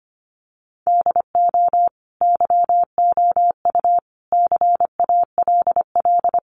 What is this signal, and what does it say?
Morse code, heard as a keyed tone, about 25 wpm, 700 Hz. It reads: DO YOU CALL
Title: Sentences from Top 1500 Words 25wpm